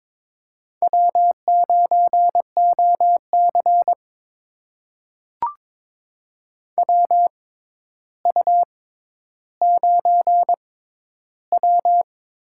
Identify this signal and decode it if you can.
Morse code, heard as a keyed tone, about 22 wpm, spaced as for 11 wpm, 700 Hz. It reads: W9OC WU9W
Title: USA Call Signs - 4 Characters - 6x Character Spacing - 22wpm